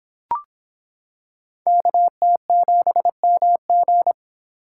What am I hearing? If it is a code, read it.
Morse code, heard as a keyed tone, about 26 wpm, 700 Hz. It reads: KT7MG